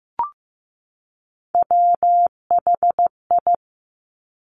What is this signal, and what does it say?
Morse code, heard as a keyed tone, about 15 wpm, 690 Hz. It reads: WHI
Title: Top Three Letter Combinations in Words 15wpm